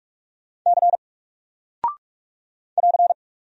Morse code, heard as a keyed tone, about 44 wpm, 700 Hz. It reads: C <AR>
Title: Single Character (Letter:Number:Punctuation:Prosign) Alt 45wpm